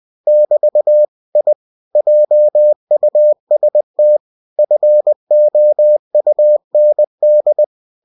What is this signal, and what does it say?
Morse code, heard as a keyed tone, about 20 wpm, 600 Hz. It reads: <BT> I JUST FOUND
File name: cw-books / winnie-the-pooh / winnie-20wpm-page-0087